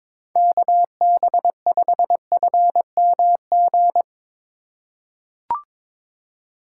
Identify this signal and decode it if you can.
Morse code, heard as a keyed tone, about 22 wpm, 700 Hz. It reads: KB5FMG
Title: USA Call Signs - 6 Characters - 3x Character Spacing - 22wpm